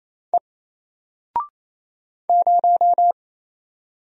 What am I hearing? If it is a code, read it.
Morse code, heard as a keyed tone, about 27 wpm, 700 Hz. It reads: E 0
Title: Single Character (Letter:Number:Punctuation:Prosign) 28wpm